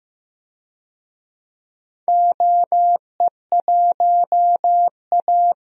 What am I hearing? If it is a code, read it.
Morse code, heard as a keyed tone, about 15 wpm, 700 Hz. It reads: OE1A
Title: DXCC-100 Call Signs - 4 Characters - Repeated 2x - 15wpm